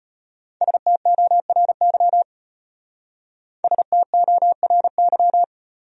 Morse code, heard as a keyed tone, about 31 wpm, 700 Hz. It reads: STORY STORY
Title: Speed-Racing Top 500 Words 25wpm